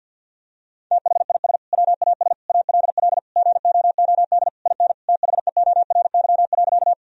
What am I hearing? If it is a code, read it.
Morse code, heard as a keyed tone, about 50 wpm, 700 Hz. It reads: THIS WAS ALL GOOD IN THEORY.
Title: Sentences from Top 1000 Words - Review Part 3 - 50wpm